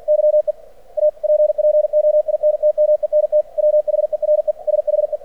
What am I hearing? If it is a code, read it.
Morse code, heard as a keyed tone, about 50 wpm, 600 Hz. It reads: YE APPOINTMENT WHERE SHE
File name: archived / CRSnet